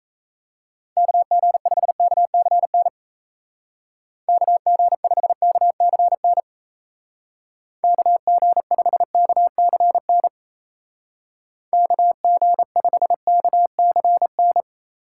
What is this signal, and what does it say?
Morse code, heard as a keyed tone, about 36 wpm, 700 Hz. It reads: KG5KCN KG5KCN KG5KCN KG5KCN